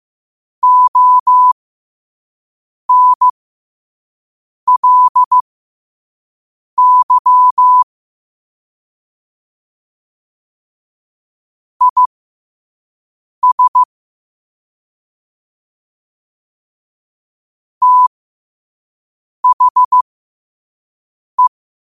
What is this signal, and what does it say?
Morse code, heard as a keyed tone, about 15 wpm, spaced as for 5 wpm, 1000 Hz. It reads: ONLY IS THE
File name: 5wpm